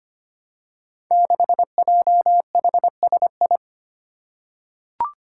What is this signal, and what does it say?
Morse code, heard as a keyed tone, about 25 wpm, 700 Hz. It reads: 6JHSI
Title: Five Challenging Letter-Number Combinations 25wpm